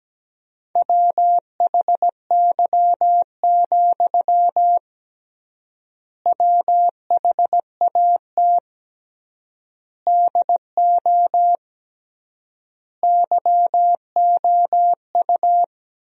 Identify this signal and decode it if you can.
Morse code, heard as a keyed tone, about 17 wpm, 700 Hz. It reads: WHY, WHAT DO YOU
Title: 3x Word-Spacing - Sentences from Top 200 Words - 17wpm